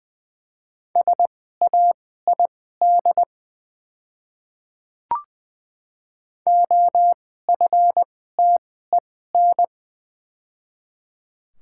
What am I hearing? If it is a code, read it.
Morse code, heard as a keyed tone, about 20 wpm, spaced as for 14 wpm, 700 Hz. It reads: SAID OFTEN
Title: 2x Character Spacing - Top 300 Words - 20wpm